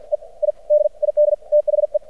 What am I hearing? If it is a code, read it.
Morse code, heard as a keyed tone, about 53 wpm, spaced as for 44 wpm, 600 Hz. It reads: E I DID THE
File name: archived / CRSnet